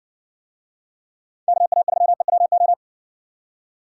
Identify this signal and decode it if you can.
Morse code, heard as a keyed tone, about 60 wpm, 700 Hz. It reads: KA3EWO